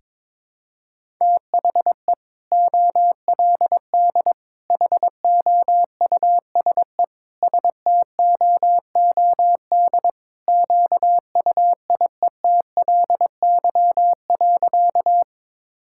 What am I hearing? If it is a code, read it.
Morse code, heard as a keyed tone, about 22 wpm, 700 Hz. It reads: THE OLD HOUSE STOOD QUIETLY.
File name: practice / 22wpm / Sentences with 2-Word Phrases - 22wpm